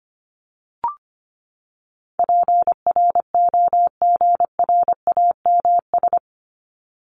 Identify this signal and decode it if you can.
Morse code, heard as a keyed tone, about 25 wpm, 700 Hz. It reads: PROGRAMS
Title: ES:S Suffix - Top 100 Words with Base Words 25wpm